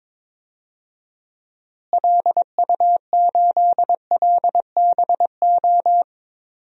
Morse code, heard as a keyed tone, about 22 wpm, 700 Hz. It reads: LU8LBO
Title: DXCC-100 Call Signs - 6 Characters - Repeated 2x - 22wpm